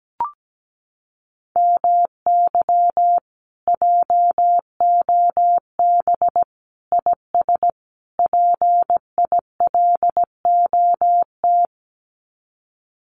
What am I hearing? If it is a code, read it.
Morse code, heard as a keyed tone, about 17 wpm, 700 Hz. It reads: MY JOB IS PILOT
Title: Common QSO Phrases Encore 17wpm